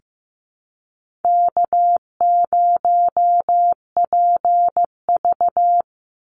Morse code, heard as a keyed tone, about 15 wpm, 700 Hz. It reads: K0PV
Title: Easy Call Signs - 15wpm